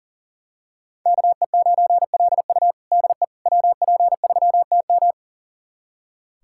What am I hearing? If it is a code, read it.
Morse code, heard as a keyed tone, about 40 wpm, 700 Hz. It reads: KE9LU DE WP3TM
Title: USA Callsign DE Callsign - 5 Characters - 40wpm